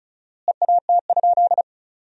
Morse code, heard as a keyed tone, about 34 wpm, 700 Hz. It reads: EAT?
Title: 4x Word-Spacing - Sentences from Top 500 Words - 35wpm